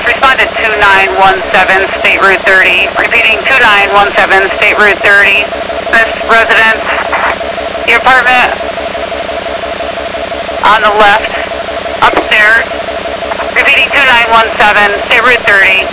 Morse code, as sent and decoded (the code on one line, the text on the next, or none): none